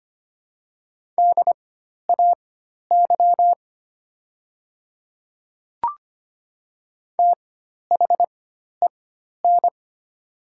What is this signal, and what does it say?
Morse code, heard as a keyed tone, about 25 wpm, spaced as for 12 wpm, 700 Hz. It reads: DAY THEN
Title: 4x Character Spacing - Top 100 Words - 25wpm